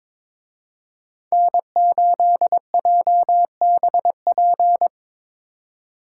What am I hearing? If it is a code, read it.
Morse code, heard as a keyed tone, about 25 wpm, 700 Hz. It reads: N8JBP
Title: Speed Racing - USA 1x3 Call Signs 22wpm